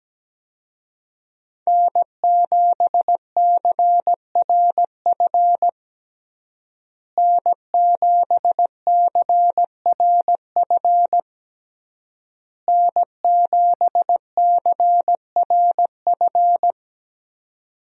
Morse code, heard as a keyed tone, about 17 wpm, 700 Hz. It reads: N7CRF N7CRF N7CRF
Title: USA Call Signs - 5 Characters - Repeated 3x - 17wpm